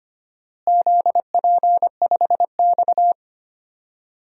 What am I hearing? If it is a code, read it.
Morse code, heard as a keyed tone, about 25 wpm, 700 Hz. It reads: ZP5X
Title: DXCC-100 Call Signs - 4 Characters - Repeated 2x - 25wpm